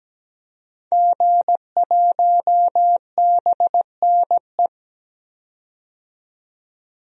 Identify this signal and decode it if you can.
Morse code, heard as a keyed tone, about 17 wpm, 700 Hz. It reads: G1BNE
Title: DXCC-100 Call Signs - 5 Characters - 17wpm